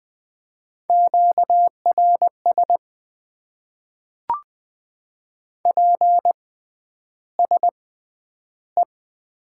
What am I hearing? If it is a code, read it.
Morse code, heard as a keyed tone, about 20 wpm, spaced as for 10 wpm, 700 Hz. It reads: QRS PSE
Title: Top 90 QSO Elements - 6x Character Spacing - 20wpm